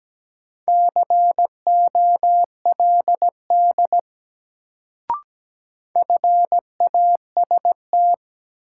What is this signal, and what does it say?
Morse code, heard as a keyed tone, about 17 wpm, 700 Hz. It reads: COLD FAST
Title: Sentences from Top 400 Words - New Word Drill - 17wpm